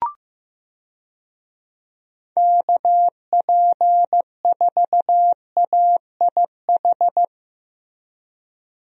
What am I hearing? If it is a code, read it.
Morse code, heard as a keyed tone, about 15 wpm, 700 Hz. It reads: KP4AIH
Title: DXCC-100 Call Signs - 6 Characters - Repeated 2x - 15wpm